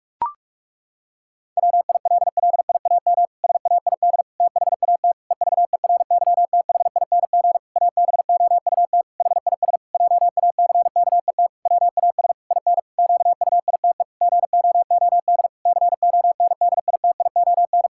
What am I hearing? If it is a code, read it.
Morse code, heard as a keyed tone, about 46 wpm, 700 Hz. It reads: WILLIAM SAID THAT EVERYTHING ABOUT HIS JACKET WAS IN QUITE GOOD CONDITION
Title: Pangram Challenge v2 - 45wpm